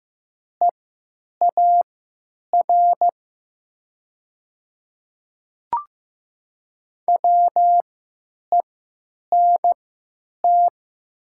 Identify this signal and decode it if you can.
Morse code, heard as a keyed tone, about 15 wpm, spaced as for 9 wpm, 700 Hz. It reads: EAR WENT
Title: 3x Character Spacing - Top 300 Words - 15wpm